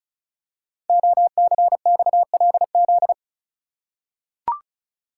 Morse code, heard as a keyed tone, about 36 wpm, 700 Hz. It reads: OCXLZ
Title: Code Groups - 5 Letters - 35wpm